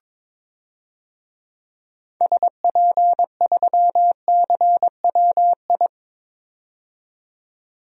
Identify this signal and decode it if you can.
Morse code, heard as a keyed tone, about 22 wpm, 700 Hz. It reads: SP3CWI
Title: DXCC-100 Call Signs - 6 Characters - 22wpm